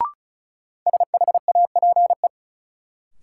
Morse code, heard as a keyed tone, about 36 wpm, 700 Hz. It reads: SHAPE
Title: Sentences from Top 500 Words - New Word Drill - 35wpm